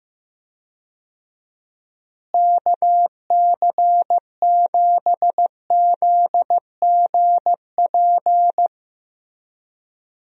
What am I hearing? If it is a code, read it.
Morse code, heard as a keyed tone, about 15 wpm, 700 Hz. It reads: KC7ZGP
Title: USA Call Signs - 6 Characters - Repeated 2x - 15wpm